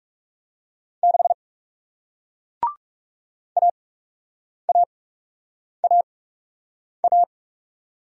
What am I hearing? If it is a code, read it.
Morse code, heard as a keyed tone, about 40 wpm, spaced as for 8 wpm, 700 Hz. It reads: 6 AAAA